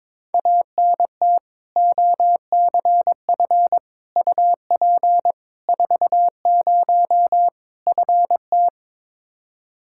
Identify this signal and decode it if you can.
Morse code, heard as a keyed tone, about 22 wpm, 700 Hz. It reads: ANT OCF UP 40 FT
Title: K1USN SST QSO Practice 22wpm